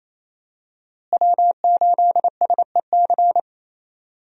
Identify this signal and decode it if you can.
Morse code, heard as a keyed tone, about 28 wpm, 700 Hz. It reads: W8SEC
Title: USA Call Signs - 5 Characters - Repeated 2x - 28wpm